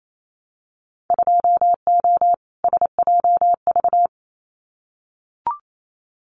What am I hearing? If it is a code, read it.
Morse code, heard as a keyed tone, about 28 wpm, 700 Hz. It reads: 2O SJV
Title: Winter Field Day Exchanges Encore 28wpm